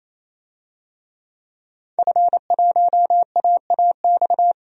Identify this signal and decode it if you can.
Morse code, heard as a keyed tone, about 28 wpm, 700 Hz. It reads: F1AAX
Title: DXCC-100 Call Signs - 5 Characters - 28wpm